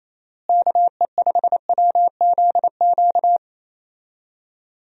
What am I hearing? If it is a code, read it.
Morse code, heard as a keyed tone, about 28 wpm, 700 Hz. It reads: KE5WZQ
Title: USA Call Signs - 6 Characters - Repeated 2x - 28wpm